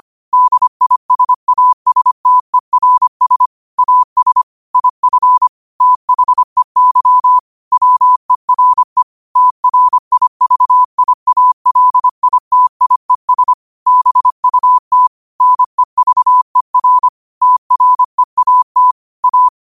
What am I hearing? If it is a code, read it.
Morse code, heard as a keyed tone, about 25 wpm, 1000 Hz. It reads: DISASTERS AS IF THEY WERE TRIVIALITIES BUT NEVER TREAT A